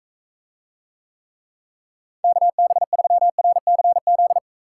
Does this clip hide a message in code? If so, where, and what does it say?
Yes: Morse code from 2.2 to 4.4 s, heard as a keyed tone, about 44 wpm, 700 Hz. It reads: KB3RCZ